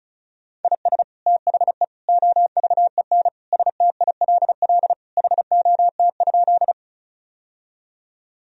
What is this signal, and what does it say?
Morse code, heard as a keyed tone, about 36 wpm, 700 Hz. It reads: IS THE OVEN STILL HOT?
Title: Sentences from Top 1000 Words - Review Part 4 - 35wpm